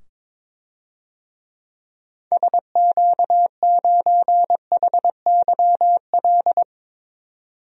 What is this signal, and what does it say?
Morse code, heard as a keyed tone, about 22 wpm, 700 Hz. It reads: SQ9HYL
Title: DXCC-100 Call Signs - 6 Characters - 22wpm